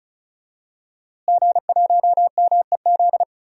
Morse code, heard as a keyed tone, about 36 wpm, 700 Hz. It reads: G1MEZ
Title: DXCC-100 Call Signs - 5 Characters - Repeated 2x - 35wpm